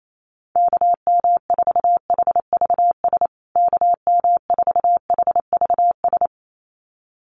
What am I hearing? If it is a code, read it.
Morse code, heard as a keyed tone, about 28 wpm, 700 Hz. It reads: KM4HVS KM4HVS
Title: USA Call Signs Repeated 2x 28wpm